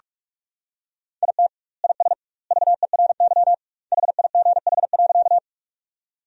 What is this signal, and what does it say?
Morse code, heard as a keyed tone, about 46 wpm, 700 Hz. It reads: IT IS VERY HIGH.
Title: 2x Word-Spacing - Sentences from Top 200 Words - 45wpm